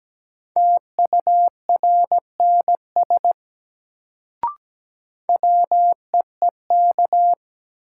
Morse code, heard as a keyed tone, about 17 wpm, 700 Hz. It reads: TURNS WEEK
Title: Sets of 5 Words Encore - New Word Drill - 17wpm